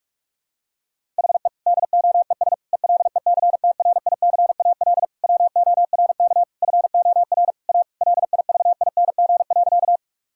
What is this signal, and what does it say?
Morse code, heard as a keyed tone, about 41 wpm, 700 Hz. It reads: HE DOES ELECTRICAL WORK FOR A LIVING.
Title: Sentences from Top 1000 Words - Review Part 6 - 45wpm